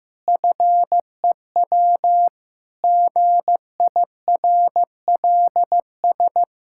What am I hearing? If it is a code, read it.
Morse code, heard as a keyed tone, about 15 wpm, 700 Hz. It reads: FEW GIRLS
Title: Pangram Challenge v2 - 15wpm